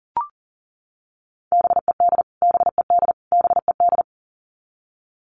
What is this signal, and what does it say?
Morse code, heard as a keyed tone, about 37 wpm, 700 Hz. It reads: BED BED BED
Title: Top 500 Words Repeated 40wpm